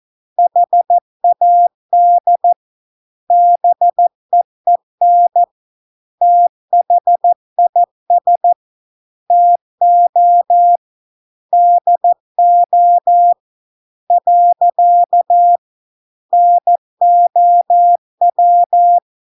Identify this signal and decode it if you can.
Morse code, heard as a keyed tone, about 14 wpm, 700 Hz. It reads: HAD BEEN THIS TO DO . NOW